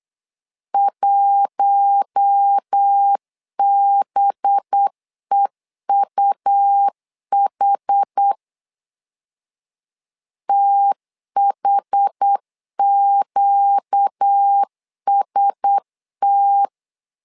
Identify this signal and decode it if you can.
Morse code, heard as a keyed tone, about 8 wpm, 800 Hz. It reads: 1BEUH THQST
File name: LEZIONI / 5 LBVC / 5 LB 8wpm 2816